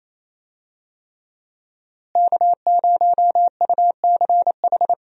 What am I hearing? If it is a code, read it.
Morse code, heard as a keyed tone, about 28 wpm, 700 Hz. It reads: K0UCH
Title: USA Call Signs - 5 Characters - Repeated 3x - 28wpm